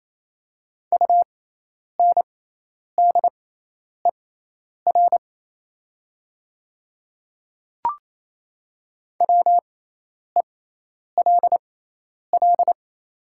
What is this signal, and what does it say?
Morse code, heard as a keyed tone, about 27 wpm, spaced as for 10 wpm, 700 Hz. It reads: UNDER WELL
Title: 6x Character Spacing - Top 200 Words - 28wpm